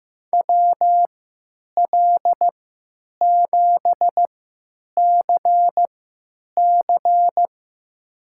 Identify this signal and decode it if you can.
Morse code, heard as a keyed tone, about 15 wpm, spaced as for 9 wpm, 700 Hz. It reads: WL7CC
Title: USA Call Signs - 5 Characters - 3x Character Spacing - 15wpm